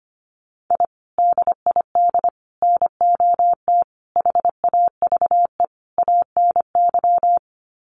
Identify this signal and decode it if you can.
Morse code, heard as a keyed tone, about 25 wpm, 700 Hz. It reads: I DID NOT HAVE ANY
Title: Sets of 5 Words Encore 25wpm